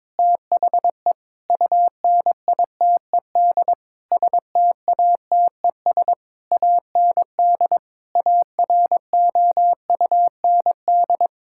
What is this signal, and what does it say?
Morse code, heard as a keyed tone, about 22 wpm, 700 Hz. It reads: THE UNITED STATES AND AROUND
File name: practice / 22wpm / Sentences with 8-Word Phrases - 22wpm